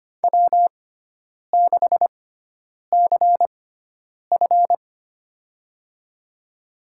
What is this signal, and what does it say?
Morse code, heard as a keyed tone, about 25 wpm, spaced as for 9 wpm, 700 Hz. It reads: W6CF
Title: USA Call Signs - 4 Characters - 6x Character Spacing - 25wpm